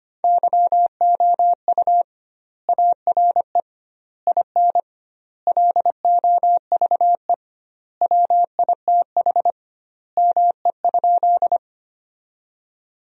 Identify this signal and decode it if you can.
Morse code, heard as a keyed tone, about 25 wpm, 700 Hz. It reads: YOU ARE IN LOVE WITH ME?
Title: 2x Word-Spacing - Sentences from Top 400 Words - 25wpm